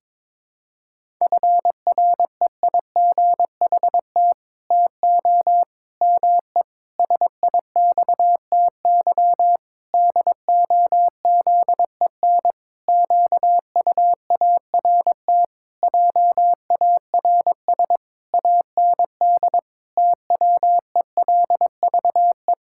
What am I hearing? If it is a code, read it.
Morse code, heard as a keyed tone, about 22 wpm, 700 Hz. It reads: FREIGHT TO ME SIXTY DOZEN QUART JARS AND TWELVE
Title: Pangram Challenge v2 - 22wpm